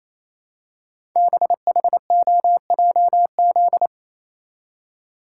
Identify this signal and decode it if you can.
Morse code, heard as a keyed tone, about 28 wpm, 700 Hz. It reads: BHOJZ
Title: Code Groups - 5 Letters - 28wpm